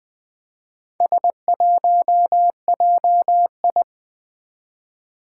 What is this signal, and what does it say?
Morse code, heard as a keyed tone, about 20 wpm, 700 Hz. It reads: S1JI
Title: Four Challenging Letter-Number Combinations 20wpm